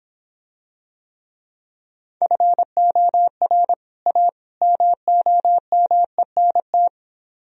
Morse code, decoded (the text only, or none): FOR A MOMENT